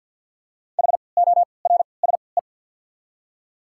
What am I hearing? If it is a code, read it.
Morse code, heard as a keyed tone, about 48 wpm, 690 Hz. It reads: H O R S E
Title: 3x Character Spacing - Top 300 Words - 50wpm